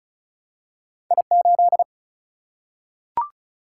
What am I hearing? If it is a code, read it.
Morse code, heard as a keyed tone, about 36 wpm, 700 Hz. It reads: I8
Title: Single-Letter Single-Number 35wpm